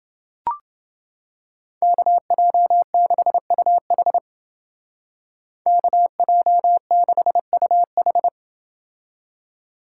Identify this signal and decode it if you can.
Morse code, heard as a keyed tone, about 29 wpm, 700 Hz. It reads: KJ6UH KJ6UH